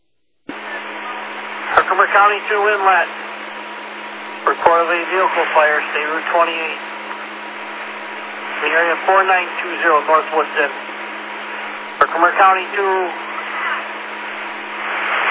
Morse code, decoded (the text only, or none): none